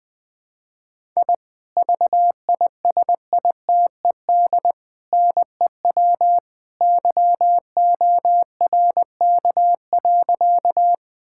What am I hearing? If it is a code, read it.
Morse code, heard as a keyed tone, about 20 wpm, 700 Hz. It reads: I VISITED NEW YORK.
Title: Sentences with 2-Word Phrases - Spoken Beginning - 20wpm